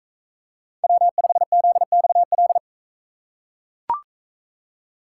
Morse code, heard as a keyed tone, about 40 wpm, 700 Hz. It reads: W5ZXL